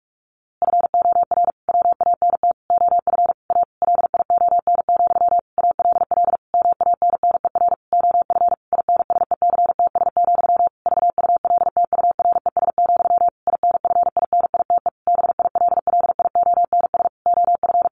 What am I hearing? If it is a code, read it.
Morse code, heard as a keyed tone, about 44 wpm, 700 Hz. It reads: FOR WANT OF A LION, ALL MANNER OF INSECTS, VULTURES, INFINITE BILLIONS OF
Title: A Sound Of Thunder 45wpm